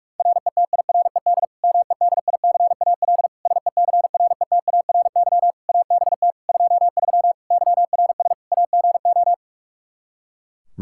Morse code, heard as a keyed tone, about 46 wpm, 700 Hz. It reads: RETIRED MEDICAL SECRETARY ABT 23 YRS AGO
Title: K1USN SST QSO Practice 45wpm